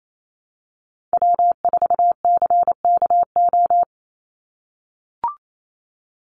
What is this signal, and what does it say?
Morse code, heard as a keyed tone, about 28 wpm, 700 Hz. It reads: W4CKO
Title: USA Call Signs 28wpm